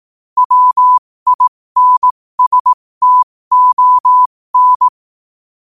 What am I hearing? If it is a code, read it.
Morse code, heard as a keyed tone, about 18 wpm, spaced as for 15 wpm, 1000 Hz. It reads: WINSTON
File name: qotdfiles / 15wpm